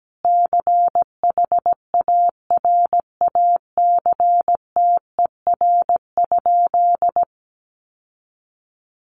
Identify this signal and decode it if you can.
Morse code, heard as a keyed tone, about 17 wpm, 700 Hz. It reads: CHARACTER?
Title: Sentences from Top 1200 Words 17wpm